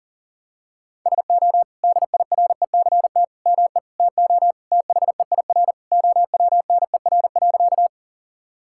Morse code, heard as a keyed tone, about 40 wpm, 700 Hz. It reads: SO DIRECT ME TO THEIR OWNER.
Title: Sentences from Top 1600 Words 40wpm